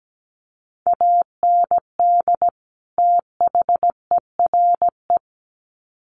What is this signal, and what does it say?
Morse code, heard as a keyed tone, about 17 wpm, 700 Hz. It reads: AND THERE
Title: Sets of 2 Words 17wpm